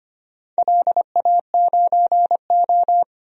Morse code, heard as a keyed tone, about 25 wpm, 700 Hz. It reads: LA9O
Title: DXCC-100 Call Signs - 4 Characters - 25wpm